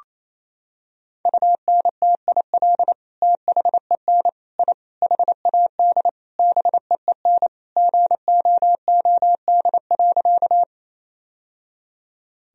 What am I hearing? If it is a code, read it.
Morse code, heard as a keyed tone, about 28 wpm, 700 Hz. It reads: UNTIL THEN I HAD BEEN GOOD.
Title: Sentences from Top 1000 Words - Review Part 1 - 28wpm